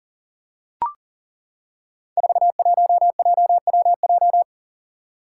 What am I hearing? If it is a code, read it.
Morse code, heard as a keyed tone, about 38 wpm, 700 Hz. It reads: 41JWJ
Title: Five Challenging Letter-Number Combinations 40wpm